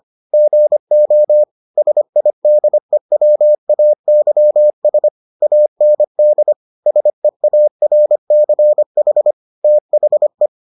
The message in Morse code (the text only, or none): GO SIDEWAYS AND SEARCH THE